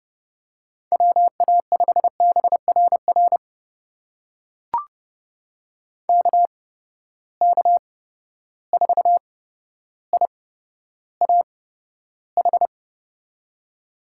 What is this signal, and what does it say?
Morse code, heard as a keyed tone, about 30 wpm, spaced as for 13 wpm, 700 Hz. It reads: WA5BRR KK4IAH
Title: USA Call Signs - 6 Characters - 8x Character Spacing - 30wpm